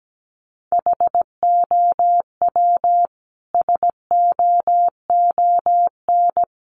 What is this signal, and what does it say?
Morse code, heard as a keyed tone, about 17 wpm, 700 Hz. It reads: HOW SOON
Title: Sentences from Top 400 Words 17wpm